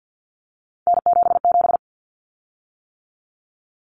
Morse code, heard as a keyed tone, about 50 wpm, 700 Hz. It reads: N77
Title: Common Cut Numbers - 3-Digit 50wpm